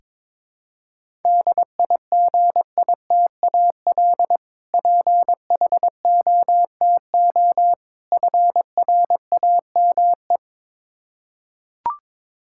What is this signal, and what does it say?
Morse code, heard as a keyed tone, about 22 wpm, 700 Hz. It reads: DIGITAL PHOTO FRAME
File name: Difficult Words with Hints - 22wpm